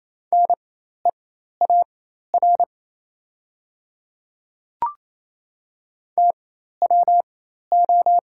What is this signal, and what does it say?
Morse code, heard as a keyed tone, about 27 wpm, spaced as for 13 wpm, 700 Hz. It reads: NEAR TWO
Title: 4x Character Spacing - Top 200 Words - 28wpm